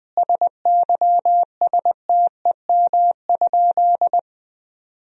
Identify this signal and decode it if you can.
Morse code, heard as a keyed tone, about 20 wpm, 700 Hz. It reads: SYSTEM?
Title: Sentences from Top 1000 Words - Review Part 3 - 20wpm